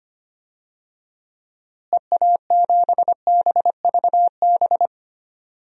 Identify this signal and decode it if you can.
Morse code, heard as a keyed tone, about 25 wpm, 700 Hz. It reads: EA7BVB
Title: DXCC-100 Call Signs - 6 Characters - Repeated 2x - 25wpm